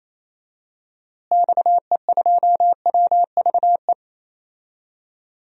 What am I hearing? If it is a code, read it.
Morse code, heard as a keyed tone, about 28 wpm, 700 Hz. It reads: XE2WVE
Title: DXCC-100 Call Signs - 6 Characters - Repeated 2x - 28wpm